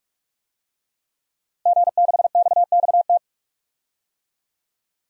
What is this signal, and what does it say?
Morse code, heard as a keyed tone, about 45 wpm, 700 Hz. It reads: G6XXT
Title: DXCC-100 Call Signs - 5 Characters - Repeated 2x - 45wpm